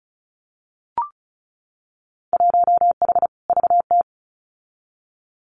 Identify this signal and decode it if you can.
Morse code, heard as a keyed tone, about 34 wpm, 700 Hz. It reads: 1H VT
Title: Winter Field Day Exchanges 35wpm